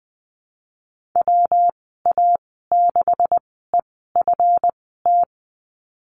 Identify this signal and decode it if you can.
Morse code, heard as a keyed tone, about 20 wpm, spaced as for 15 wpm, 700 Hz. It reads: WA6EFT
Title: 161 Lesson 26g - Letter g - Call Signs 20wpm